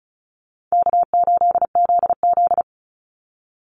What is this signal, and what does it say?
Morse code, heard as a keyed tone, about 34 wpm, 700 Hz. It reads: K8ZZ
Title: Easy Call Signs Repeated 3x - 35wpm